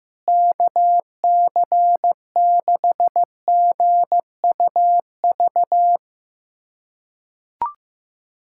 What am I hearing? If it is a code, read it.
Morse code, heard as a keyed tone, about 15 wpm, 700 Hz. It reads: KC6GUV
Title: USA Callsign DE Callsign 15wpm